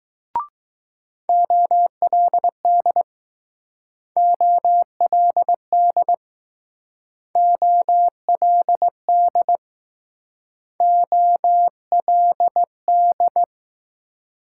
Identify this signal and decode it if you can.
Morse code, heard as a keyed tone, about 20 wpm, 700 Hz. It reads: OLD OLD OLD OLD